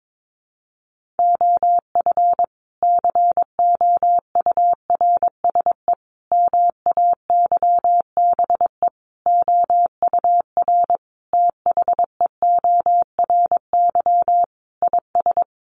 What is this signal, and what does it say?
Morse code, heard as a keyed tone, about 22 wpm, 700 Hz. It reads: OF COURSE MAYBE OUR THEORY IS
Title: A Sound Of Thunder 22wpm